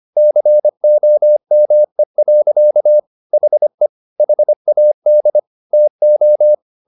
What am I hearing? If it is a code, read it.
Morse code, heard as a keyed tone, about 25 wpm, 600 Hz. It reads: COME. HE HAD TO